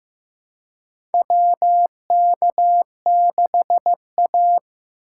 Morse code, heard as a keyed tone, about 15 wpm, 700 Hz. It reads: WK6A
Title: USA Call Signs - 4x Character Spacing - 15wpm